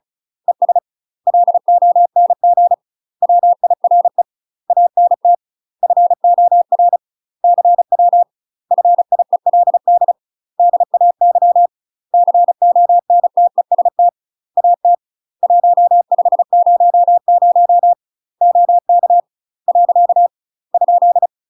Morse code, heard as a keyed tone, about 34 wpm, 700 Hz. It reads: ES LONG WIRE ANT FOR CW FIELD DAY CONTEST AT 1500 OK . ?
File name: UK MASTER QSOs TRAINING / 35WPM / UK MASTER QSOs 0021